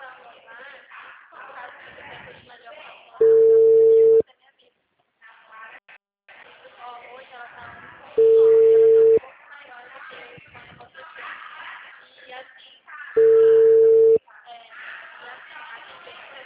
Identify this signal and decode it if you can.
Ringback tone (China, 450 Hz), with speech.
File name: out- -2050- - - .51381